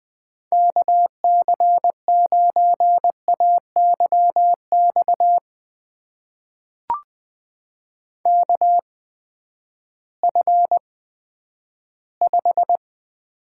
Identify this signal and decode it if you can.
Morse code, heard as a keyed tone, about 20 wpm, spaced as for 11 wpm, 700 Hz. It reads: KC9AYX KF5
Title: USA Call Signs - 6 Characters - 8x Character Spacing - 20wpm